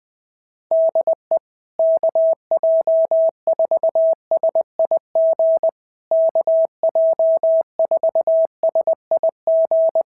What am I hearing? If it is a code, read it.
Morse code, heard as a keyed tone, about 18 wpm, 650 Hz. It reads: DE KJ4SIG KJ4SIG
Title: SOTA QSO Practice 20wpm